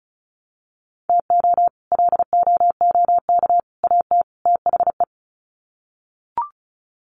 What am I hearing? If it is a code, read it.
Morse code, heard as a keyed tone, about 34 wpm, 700 Hz. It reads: TO LOOK AT THE
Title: Sets of 4 Words 35wpm